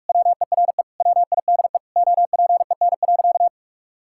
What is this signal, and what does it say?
Morse code, heard as a keyed tone, about 46 wpm, 700 Hz. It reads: WERE WIDE OPEN.
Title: Sentences from Top 1000 Words - Review Part 6 - 45wpm